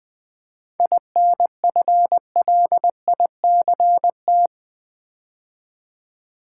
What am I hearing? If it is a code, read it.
Morse code, heard as a keyed tone, about 20 wpm, 700 Hz. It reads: INFLICT
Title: 7 Letter Words - First 3 Letters Given - 20wpm